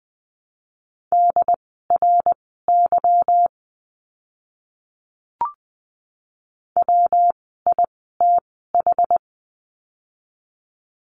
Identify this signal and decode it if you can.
Morse code, heard as a keyed tone, about 19 wpm, spaced as for 14 wpm, 700 Hz. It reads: DRY WITH
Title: 146 Lesson 24f - Letter y - Words 20wpm